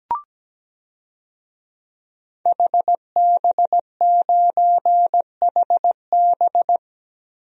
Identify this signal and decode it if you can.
Morse code, heard as a keyed tone, about 17 wpm, 700 Hz. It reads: HB9HB
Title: DXCC-100 Call Signs - 5 Characters - Repeated 2x - 17wpm